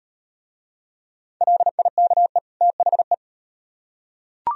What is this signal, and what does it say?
Morse code, heard as a keyed tone, about 37 wpm, 700 Hz. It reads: LIKE THE